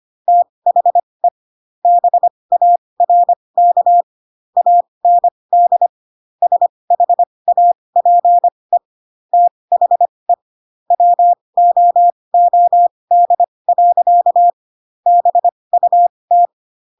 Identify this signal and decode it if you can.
Morse code, heard as a keyed tone, about 25 wpm, spaced as for 20 wpm, 700 Hz. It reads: THE BARK AND SHAPE THE WOOD. BUT
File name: Pinocchio / Pinocchio ch1 15 25 ik1hge